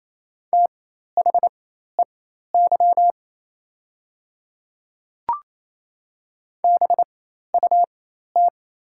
Morse code, heard as a keyed tone, about 28 wpm, spaced as for 13 wpm, 700 Hz. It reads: THEY BUT
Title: 4x Character Spacing - Top 100 Words - 28wpm